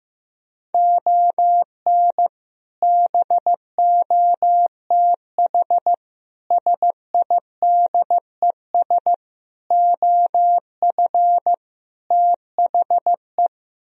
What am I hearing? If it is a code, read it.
Morse code, heard as a keyed tone, about 15 wpm, 700 Hz. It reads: ON BOTH SIDES OF THE
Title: Speed-Racing - Sets of 5 Words 15wpm